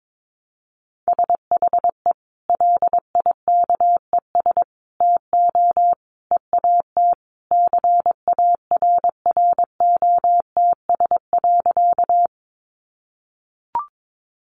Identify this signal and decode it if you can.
Morse code, heard as a keyed tone, about 22 wpm, 700 Hz. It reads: SHE LIKES TO EAT CARROTS.
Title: Easy Phrases 22wpm